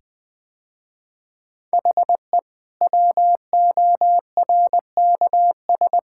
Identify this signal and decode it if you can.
Morse code, heard as a keyed tone, about 20 wpm, 700 Hz. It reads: HE WORKS
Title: Sentences with 8-Word Phrases - Spoken Beginning - 20wpm